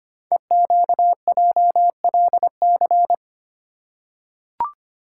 Morse code, heard as a keyed tone, about 25 wpm, 700 Hz. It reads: EQJLC
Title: Code Groups - 5 Letter:Numbers - 25wpm